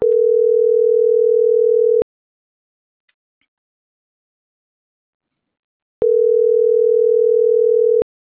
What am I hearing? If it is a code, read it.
Ringback tone (North America), 440 + 480 Hz.